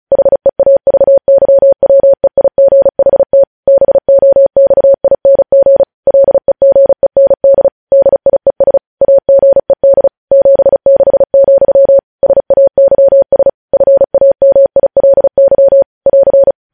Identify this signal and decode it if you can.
Morse code, heard as a keyed tone, about 36 wpm, 550 Hz. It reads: HEAVYWEIGHT BOXING LEGEND DIES AGED 76, SAYS FAMILY <AR>
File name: podcast / News headlines-1x-35x35